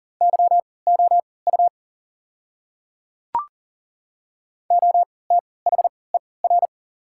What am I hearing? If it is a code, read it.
Morse code, heard as a keyed tone, about 37 wpm, spaced as for 22 wpm, 690 Hz. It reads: YOU OTHER
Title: 3x Character Spacing - Top 100 Words - 40wpm